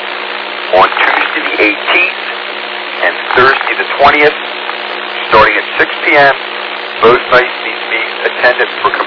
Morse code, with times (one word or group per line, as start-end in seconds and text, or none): none